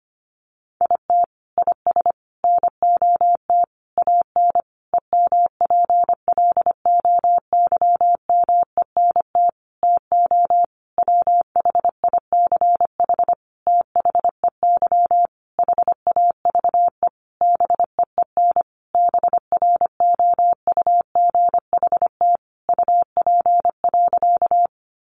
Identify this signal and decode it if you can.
Morse code, heard as a keyed tone, about 25 wpm, 700 Hz. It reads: IT IS NOT AN EMPLOYMENT TO WHICH THEY HAVE BEEN BROUGHT UP.
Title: Sentences from Top 1500 Words 25wpm